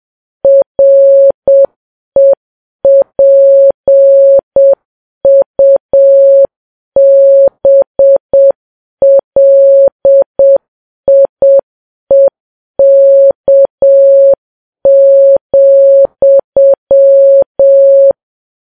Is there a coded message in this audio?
Yes: Morse code from 0.4 to 18.1 s, heard as a keyed tone, about 7 wpm, 550 Hz. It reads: REPUBLIEK,